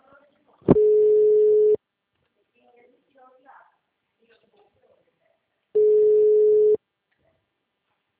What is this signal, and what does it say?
Ringback tone (Europe, 425 Hz), with speech.